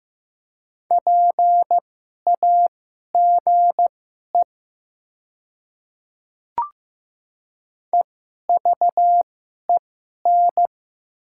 Morse code, heard as a keyed tone, about 15 wpm, spaced as for 11 wpm, 700 Hz. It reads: PAGE EVEN
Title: 2x Character Spacing - Top 300 Words - 15wpm